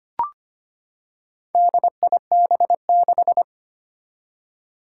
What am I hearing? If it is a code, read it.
Morse code, heard as a keyed tone, about 25 wpm, 700 Hz. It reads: DIB6
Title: Four Challenging Letter-Number Combinations 25wpm